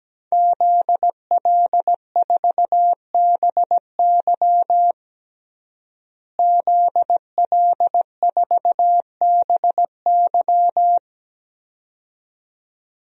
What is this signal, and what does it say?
Morse code, heard as a keyed tone, about 17 wpm, 700 Hz. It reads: ZL4BY ZL4BY
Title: DXCC-100 Call Signs - 5 Characters - Repeated 2x - 17wpm